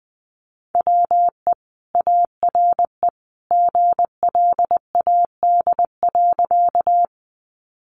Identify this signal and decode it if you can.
Morse code, heard as a keyed tone, about 19 wpm, 700 Hz. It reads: WE ARE GLAD.
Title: Sentences from Top 800 Words 20wpm